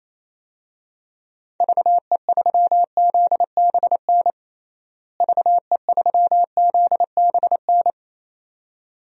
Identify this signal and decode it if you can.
Morse code, heard as a keyed tone, about 28 wpm, 700 Hz. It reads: VE3ZBN VE3ZBN
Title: DXCC-100 Call Signs - 6 Characters - Repeated 2x - 28wpm